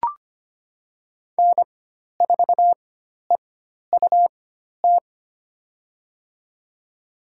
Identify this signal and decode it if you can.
Morse code, heard as a keyed tone, about 25 wpm, spaced as for 12 wpm, 700 Hz. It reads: N4EUT
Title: USA Call Signs - 5 Characters - 4x Character Spacing - 25wpm